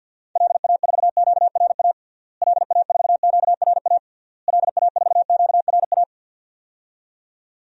Morse code, heard as a keyed tone, about 52 wpm, 700 Hz. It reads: LA4QRA LA4QRA LA4QRA
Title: DXCC-100 Call Signs - 6 Characters - Repeated 3x - 50wpm